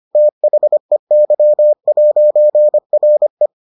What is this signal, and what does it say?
Morse code, heard as a keyed tone, about 25 wpm, 600 Hz. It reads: THEY'RE